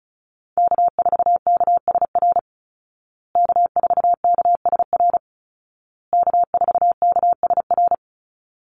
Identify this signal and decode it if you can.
Morse code, heard as a keyed tone, about 34 wpm, 700 Hz. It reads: K4KSR K4KSR K4KSR
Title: Call Signs Repeated 2x - 35wpm